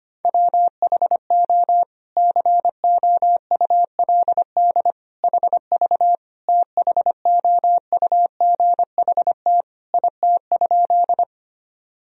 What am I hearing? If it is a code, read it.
Morse code, heard as a keyed tone, about 25 wpm, 700 Hz. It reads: WHO COULD HV THOUGHT IT?
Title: Sentences from Top 300 Words and Abbreviations 25wpm